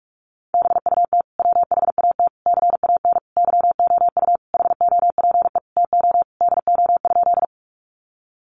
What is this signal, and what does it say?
Morse code, heard as a keyed tone, about 44 wpm, 700 Hz. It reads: BUT WHAT CAN YOU HOPE TO DO?
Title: Sentences from Top 1000 Words 45wpm